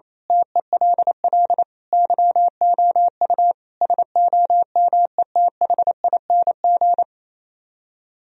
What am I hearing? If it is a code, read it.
Morse code, heard as a keyed tone, about 27 wpm, 700 Hz. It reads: TELL YOU SOMETHING
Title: Speed-Racing - Sets of 5 Words 28wpm